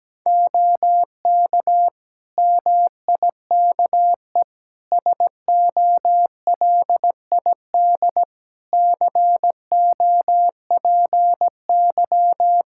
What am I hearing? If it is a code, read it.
Morse code, heard as a keyed tone, about 17 wpm, 700 Hz. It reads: OK MIKE SOLID COPY
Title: Practice QSO - Exchange 3 - More Info - 17wpm